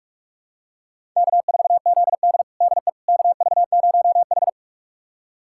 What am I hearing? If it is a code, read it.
Morse code, heard as a keyed tone, about 45 wpm, 700 Hz. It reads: K4ZD DE KV0H
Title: USA Callsign DE Callsign - 4 Characters - 45wpm